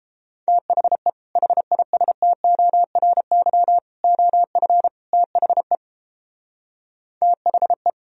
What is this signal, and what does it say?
Morse code, heard as a keyed tone, about 29 wpm, 700 Hz. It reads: THE HISTORY OF THE THE